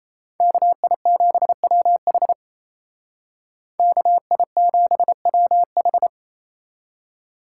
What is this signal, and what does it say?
Morse code, heard as a keyed tone, about 36 wpm, 700 Hz. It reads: KI7WH KI7WH